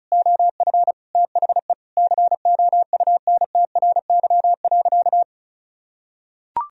Morse code, heard as a keyed tone, about 34 wpm, 700 Hz. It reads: OF THE COUNTRY.
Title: Sentences from Top 1000 Words - Review Part 2 - 35wpm